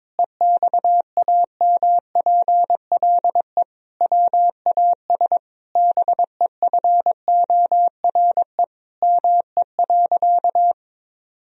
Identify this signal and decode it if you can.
Morse code, heard as a keyed tone, about 22 wpm, 700 Hz. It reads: EXAMPLE WAS BEFORE ME.
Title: Sentences from Top 1000 Words - Review Part 1 - 22wpm